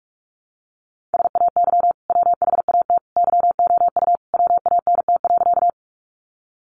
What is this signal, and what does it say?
Morse code, heard as a keyed tone, about 44 wpm, 700 Hz. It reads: SAY WHAT YOU WANT.
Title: Sentences from Top 200 Words 45wpm